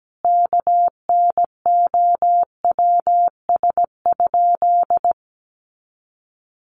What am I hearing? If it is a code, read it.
Morse code, heard as a keyed tone, about 17 wpm, 700 Hz. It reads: KNOWS?
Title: A Sound Of Thunder 17wpm